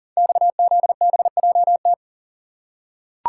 Morse code, heard as a keyed tone, about 39 wpm, 700 Hz. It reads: XZBJT
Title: Code Groups - 5 Letter:Numbers - 40wpm